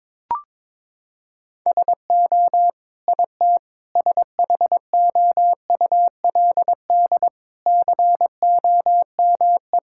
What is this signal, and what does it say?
Morse code, heard as a keyed tone, about 22 wpm, 700 Hz. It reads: SO IT SHOULD COME
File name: Sentences with 9-Word Phrases - 22wpm